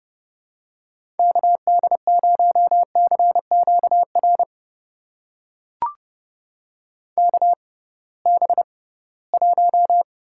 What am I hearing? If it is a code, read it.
Morse code, heard as a keyed tone, about 30 wpm, spaced as for 19 wpm, 700 Hz. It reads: KD0CQR KB1
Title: USA Call Signs - 6 Characters - 6x Character Spacing - 30wpm